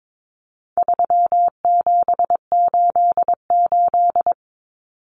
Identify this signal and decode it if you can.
Morse code, heard as a keyed tone, about 22 wpm, 700 Hz. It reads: 3788
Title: Common Cut Numbers - 4-Digit 22wpm